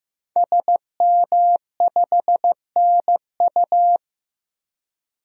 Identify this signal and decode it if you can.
Morse code, heard as a keyed tone, about 15 wpm, 700 Hz. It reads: SM5NU
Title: DXCC-100 Call Signs - 5 Characters - 15wpm